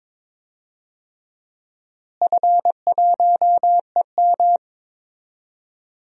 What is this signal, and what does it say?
Morse code, heard as a keyed tone, about 22 wpm, 700 Hz. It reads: F1EM
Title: DXCC-100 Call Signs - 4 Characters - Repeated 3x - 22wpm